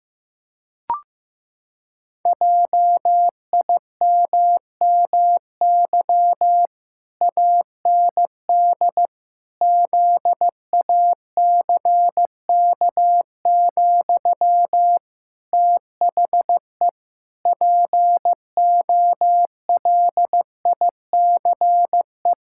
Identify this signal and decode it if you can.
Morse code, heard as a keyed tone, about 16 wpm, 700 Hz. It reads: JIMMY AND ZACK, THE POLICE
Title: Pangram Challenge v2 - 15wpm